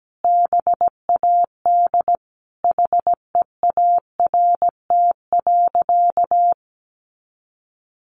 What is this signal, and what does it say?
Morse code, heard as a keyed tone, about 17 wpm, 700 Hz. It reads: BAD HEART.
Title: Sentences from Top 1200 Words 17wpm